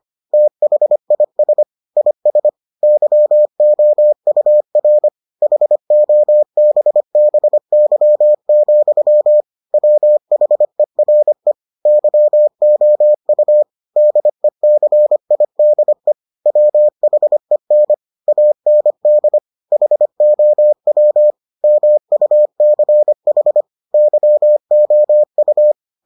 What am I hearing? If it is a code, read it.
Morse code, heard as a keyed tone, about 25 wpm, 600 Hz. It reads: THIS IS YOUR HOBBY, WHERE YOU DECIDE WHEN AND HOW MUCH YOU